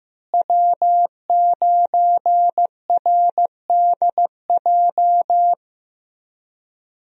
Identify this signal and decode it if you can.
Morse code, heard as a keyed tone, about 15 wpm, 700 Hz. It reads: W9RDJ
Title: USA Call Signs - 5 Characters - Repeated 2x - 15wpm